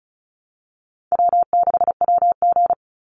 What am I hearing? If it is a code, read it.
Morse code, heard as a keyed tone, about 34 wpm, 700 Hz. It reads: W6WG
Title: Easy Call Signs Repeated 3x - 35wpm